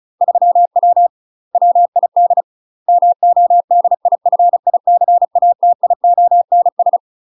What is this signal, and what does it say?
Morse code, heard as a keyed tone, about 36 wpm, 700 Hz. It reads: 3W WID MODIFICATIONS
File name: UK MASTER QSOs TRAINING / 35WPM / UK MASTER QSOs 0023